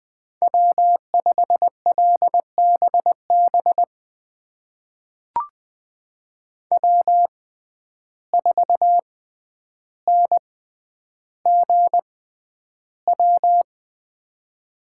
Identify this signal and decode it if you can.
Morse code, heard as a keyed tone, about 20 wpm, spaced as for 10 wpm, 700 Hz. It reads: W5LBB W4NGW
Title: USA Call Signs - 5 Characters - 6x Character Spacing - 20wpm